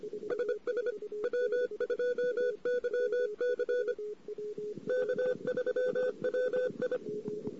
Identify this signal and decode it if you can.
Morse code, heard as a keyed tone, about 24 wpm, 500 Hz. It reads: SS W2YC K3WI
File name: CW / 2008-ss-cw-k5tr-op-n5rz -14:00